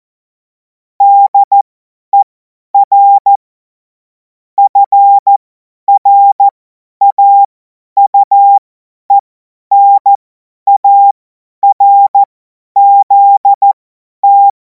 Morse code, heard as a keyed tone, about 14 wpm, spaced as for 10 wpm, 800 Hz. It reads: DER FRAUENARZT
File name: Morsekurs / Witze / Witz 14. Juli 2026 50 F70